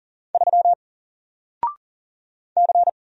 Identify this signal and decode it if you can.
Morse code, heard as a keyed tone, about 40 wpm, 700 Hz. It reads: 3 C
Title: Single Letter-Number 40wpm